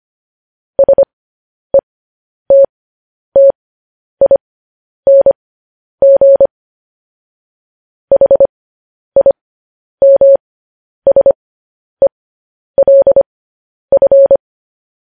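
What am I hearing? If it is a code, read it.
Morse code, heard as a keyed tone, about 25 wpm, spaced as for 10 wpm, 550 Hz. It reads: SETTING HIMSELF
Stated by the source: Morse code news headlines broadcast.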